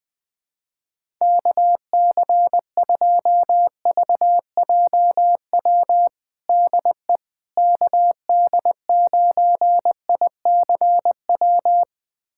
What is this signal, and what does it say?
Morse code, heard as a keyed tone, about 20 wpm, 700 Hz. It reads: KC2VJW DE KD9ICW
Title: USA Callsign DE Callsign 20wpm